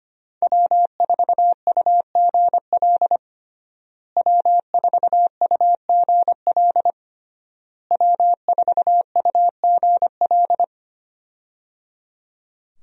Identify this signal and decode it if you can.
Morse code, heard as a keyed tone, about 25 wpm, 700 Hz. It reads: W4UGL W4UGL W4UGL
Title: USA Call Signs - 5 Characters - Repeated 3x - 25wpm